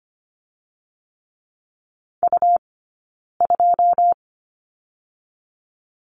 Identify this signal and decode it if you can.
Morse code, heard as a keyed tone, about 25 wpm, 700 Hz. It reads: U 2
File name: Lesson 17d - Letter u - Two Characters 25wpm